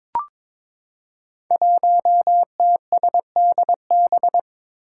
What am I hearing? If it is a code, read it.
Morse code, heard as a keyed tone, about 22 wpm, 700 Hz. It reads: 1TSDB
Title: Five Challenging Letter-Number Combinations 22wpm